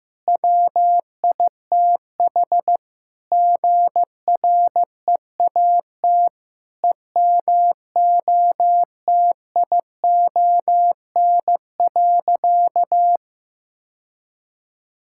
Morse code, heard as a keyed tone, about 15 wpm, 700 Hz. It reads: WITH GREAT EMOTION.
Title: Sentences from Top 1600 Words 15wpm